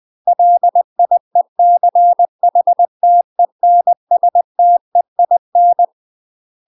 Morse code, heard as a keyed tone, about 20 wpm, 700 Hz. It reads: LIECHTENSTEIN
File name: UK MASTER QSOs TRAINING / 20WPM / UK MASTER QSOs 0055